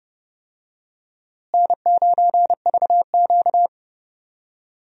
Morse code, heard as a keyed tone, about 30 wpm, 700 Hz. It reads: N9VQ
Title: USA Call Signs - 4 Characters - Repeated 3x - 30wpm